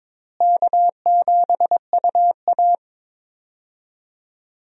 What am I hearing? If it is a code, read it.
Morse code, heard as a keyed tone, about 22 wpm, 700 Hz. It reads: K7UA
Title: USA Call Signs - 4 Characters - Repeated 2x - 22wpm